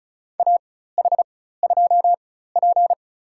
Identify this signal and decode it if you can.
Morse code, heard as a keyed tone, about 36 wpm, spaced as for 16 wpm, 700 Hz. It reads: AH2P
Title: USA Call Signs - 4 Characters - 4x Character Spacing - 35wpm